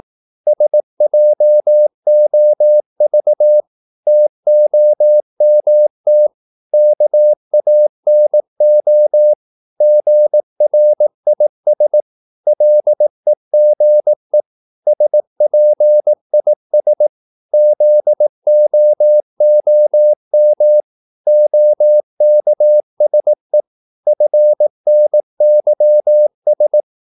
Morse code, heard as a keyed tone, about 18 wpm, 600 Hz. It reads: SJOV TOMT KANO GRIS LEGE SPIS ZOOM OKSE FNYS
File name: Korte ord DK 18wpm